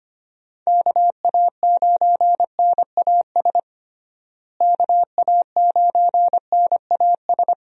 Morse code, heard as a keyed tone, about 25 wpm, 700 Hz. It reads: KA9NAS KA9NAS
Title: USA Call Signs - 6 Characters - Repeated 2x - 25wpm